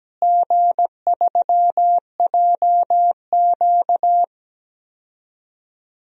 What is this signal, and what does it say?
Morse code, heard as a keyed tone, about 17 wpm, 700 Hz. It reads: G3JQ
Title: DXCC-100 Call Signs - 4 Characters - Repeated 3x - 17wpm